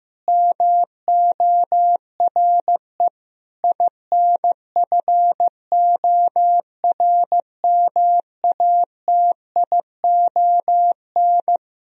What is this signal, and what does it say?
Morse code, heard as a keyed tone, about 15 wpm, 700 Hz. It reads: MORE INFORMATION
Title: Sentences from Top 1000 Words - Review Part 3 - 15wpm